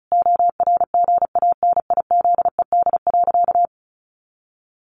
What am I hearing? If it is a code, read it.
Morse code, heard as a keyed tone, about 34 wpm, 700 Hz. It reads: ORGANIZED.
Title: Sentences from Top 1300 Words 35wpm